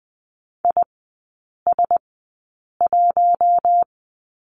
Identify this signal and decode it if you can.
Morse code, heard as a keyed tone, about 20 wpm, spaced as for 8 wpm, 700 Hz. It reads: IS1
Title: 074 Lesson 13e - Letter L - Three Characters 20wpm